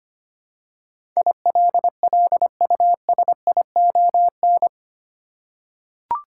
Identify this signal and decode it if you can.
Morse code, heard as a keyed tone, about 25 wpm, 700 Hz. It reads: ILLUSION
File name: 8 Letter Words - First 3 Letters Given - 25wpm